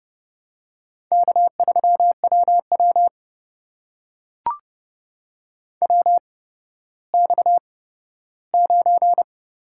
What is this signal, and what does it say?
Morse code, heard as a keyed tone, about 30 wpm, spaced as for 15 wpm, 700 Hz. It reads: K3WW WX9
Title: USA Call Signs - 4 Characters - 8x Character Spacing - 30wpm